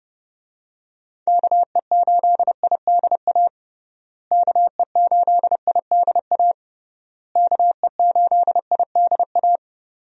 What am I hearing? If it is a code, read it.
Morse code, heard as a keyed tone, about 30 wpm, 700 Hz. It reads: KE8IDA KE8IDA KE8IDA
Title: USA Call Signs - 6 Characters - Repeated 3x - 30wpm